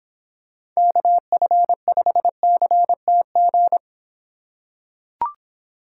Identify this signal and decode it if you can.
Morse code, heard as a keyed tone, about 26 wpm, 700 Hz. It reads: KF5CTG